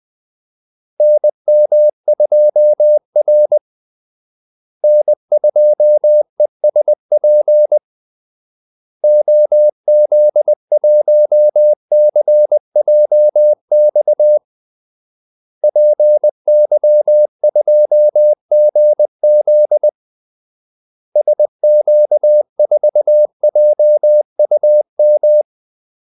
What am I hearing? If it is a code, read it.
Morse code, heard as a keyed tone, about 20 wpm, 600 Hz. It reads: NM2R N2ESP OZ1CJX PY2GZ SQ4JUM
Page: Kaldesignaler 20 wpm